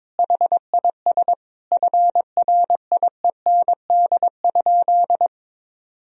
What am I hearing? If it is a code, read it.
Morse code, heard as a keyed tone, about 22 wpm, 700 Hz. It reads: HIS FRIEND?
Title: Sentences from Top 1000 Words - Review Part 3 - 22wpm